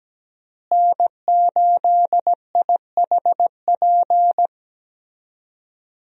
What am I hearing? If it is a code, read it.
Morse code, heard as a keyed tone, about 17 wpm, 700 Hz. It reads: N8IHP
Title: USA Call Signs - 5 Characters - Repeated 3x - 17wpm